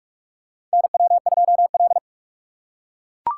Morse code, heard as a keyed tone, about 44 wpm, 700 Hz. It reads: NW2L
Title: USA Call Signs - 4 Characters - 2x Character Spacing - 45wpm